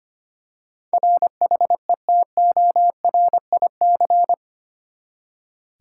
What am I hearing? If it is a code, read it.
Morse code, heard as a keyed tone, about 25 wpm, 700 Hz. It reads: RHETORIC
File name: Eight Letter Words - 25wpm